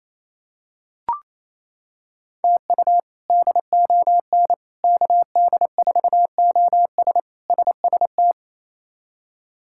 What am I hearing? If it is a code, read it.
Morse code, heard as a keyed tone, about 28 wpm, 700 Hz. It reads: TU DON KD4OS SST
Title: K1USN SST QSO Practice - 28wpm